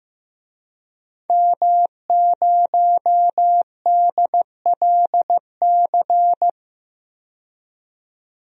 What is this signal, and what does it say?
Morse code, heard as a keyed tone, about 15 wpm, 700 Hz. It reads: M0DLC
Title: DXCC-100 Call Signs - 5 Characters - 15wpm